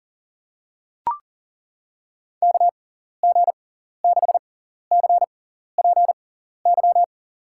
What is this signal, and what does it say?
Morse code, heard as a keyed tone, about 40 wpm, spaced as for 14 wpm, 700 Hz. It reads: KG6CPY
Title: USA Call Signs - 6x Character Spacing - 40wpm